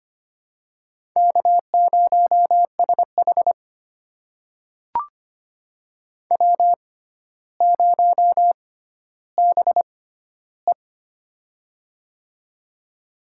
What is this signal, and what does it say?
Morse code, heard as a keyed tone, about 25 wpm, spaced as for 13 wpm, 700 Hz. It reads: K0SH W0BE
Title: USA Call Signs - 4 Characters - 6x Character Spacing - 25wpm